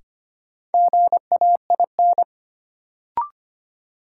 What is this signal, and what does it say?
Morse code, heard as a keyed tone, about 25 wpm, 700 Hz. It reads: GAIN
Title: Sentences from Top 1000 Words - New Word Drill - 25wpm